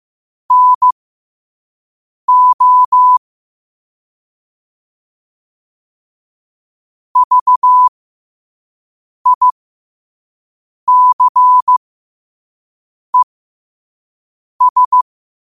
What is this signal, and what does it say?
Morse code, heard as a keyed tone, about 15 wpm, spaced as for 5 wpm, 1000 Hz. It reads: NO VICES
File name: qotdfiles / 5wpm